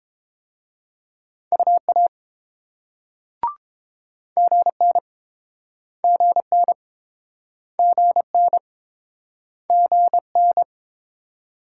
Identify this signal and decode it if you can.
Morse code, heard as a keyed tone, about 31 wpm, 690 Hz. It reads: UA GN GN GN GN